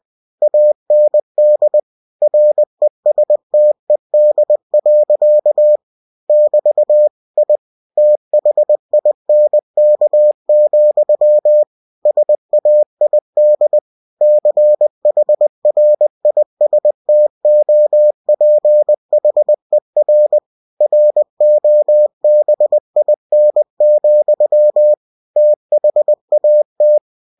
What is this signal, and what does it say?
Morse code, heard as a keyed tone, about 20 wpm, 600 Hz. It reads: AND RESTED. <BT> I THINK, SAID CHRISTOPHER ROBIN, THAT
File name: cw-books / winnie-the-pooh / winnie-20wpm-page-0083